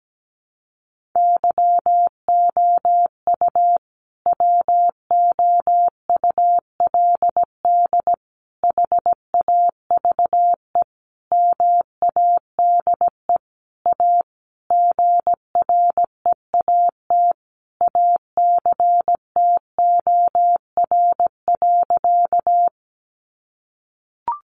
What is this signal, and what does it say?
Morse code, heard as a keyed tone, about 17 wpm, 700 Hz. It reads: YOU WOULD HAVE MADE A GREAT ACTOR.
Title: Sentences from Top 1500 Words 17wpm